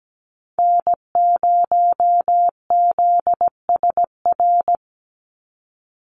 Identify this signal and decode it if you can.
Morse code, heard as a keyed tone, about 17 wpm, 700 Hz. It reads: N0ZSR
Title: USA Call Signs 17wpm